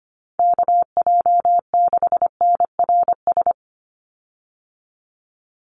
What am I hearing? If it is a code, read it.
Morse code, heard as a keyed tone, about 25 wpm, 700 Hz. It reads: KJ6NRS
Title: USA Call Signs 25wpm